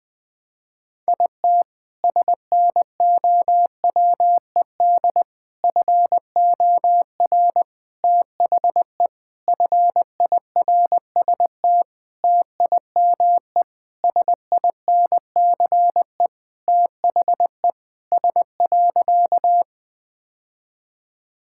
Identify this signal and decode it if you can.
Morse code, heard as a keyed tone, about 20 wpm, 700 Hz. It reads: IT SNOWED FOR THE FIRST TIME SINCE THE S.
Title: Sentences with 6-Word Phrases - 20wpm